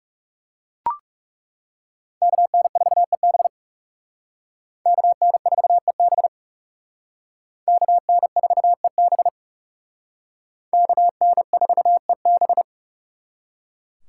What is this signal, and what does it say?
Morse code, heard as a keyed tone, about 39 wpm, 700 Hz. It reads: KN4EB KN4EB KN4EB KN4EB